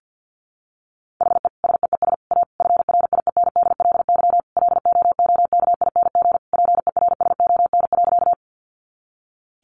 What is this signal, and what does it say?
Morse code, heard as a keyed tone, about 48 wpm, 700 Hz. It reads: HE SEES A FRIENDLY LOOKING PERSON.
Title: Easy Phrases 50wpm